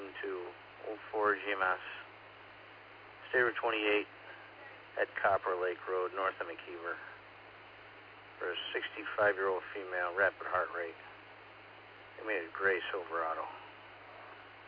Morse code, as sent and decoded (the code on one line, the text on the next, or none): none